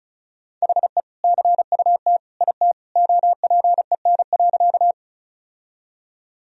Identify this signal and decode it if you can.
Morse code, heard as a keyed tone, about 36 wpm, 700 Hz. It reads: HE CUT IT OPEN.
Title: Sentences from Top 1000 Words - Review Part 1 - 35wpm